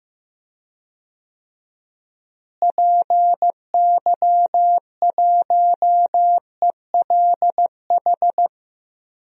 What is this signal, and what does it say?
Morse code, heard as a keyed tone, about 15 wpm, 700 Hz. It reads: PY1ELH
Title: DXCC-100 Call Signs - 6 Characters - Repeated 3x - 15wpm